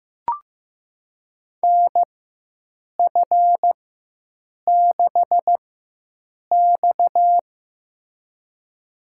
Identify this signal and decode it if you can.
Morse code, heard as a keyed tone, about 15 wpm, spaced as for 7 wpm, 700 Hz. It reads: NF6X
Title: USA Call Signs - 4x Character Spacing - 15wpm